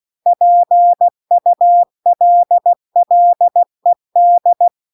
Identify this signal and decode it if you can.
Morse code, heard as a keyed tone, about 16 wpm, 700 Hz. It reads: PULLED
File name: CW-The-Big-Twoheaded-River-2-16WPM